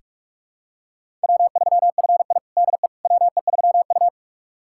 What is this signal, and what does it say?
Morse code, heard as a keyed tone, about 43 wpm, 700 Hz. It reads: W3FI DE WE3U
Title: USA Callsign DE Callsign - 4 Characters - 45wpm